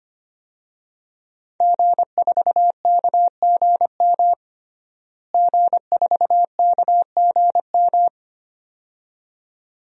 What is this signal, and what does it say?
Morse code, heard as a keyed tone, about 25 wpm, 700 Hz. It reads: G4KGM G4KGM
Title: DXCC-100 Call Signs - 5 Characters - Repeated 2x - 25wpm